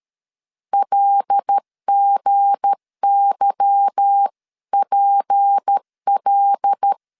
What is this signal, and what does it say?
Morse code, heard as a keyed tone, about 13 wpm, 800 Hz. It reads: LGYPL
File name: LEZIONI / 8 YJ45 / 8 Y 12wpm 2228